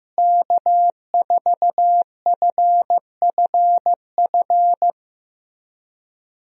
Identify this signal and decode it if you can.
Morse code, heard as a keyed tone, about 15 wpm, 700 Hz. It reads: K4FFF
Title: USA Call Signs - 5 Characters - Repeated 3x - 15wpm